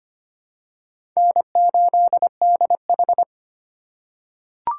Morse code, heard as a keyed tone, about 25 wpm, 700 Hz. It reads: N8DH
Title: USA Call Signs - 4 Characters - 6x Character Spacing - 25wpm